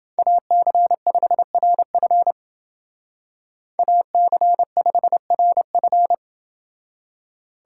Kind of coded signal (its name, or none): Morse code